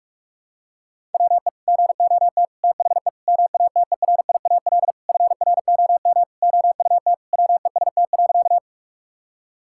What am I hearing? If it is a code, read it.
Morse code, heard as a keyed tone, about 41 wpm, 700 Hz. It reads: WE GOT THE MATERIAL FROM OUT WEST.
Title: Sentences from Top 1000 Words - Review Part 5 - 45wpm